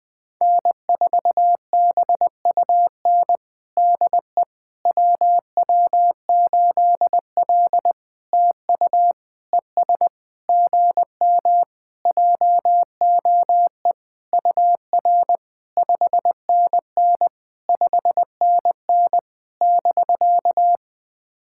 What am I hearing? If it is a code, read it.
Morse code, heard as a keyed tone, about 20 wpm, 700 Hz. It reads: N4BUN DE WW8L TU ES GM JOE UR 5NN 5NN <BK>
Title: POTA QSO Practice - 20wpm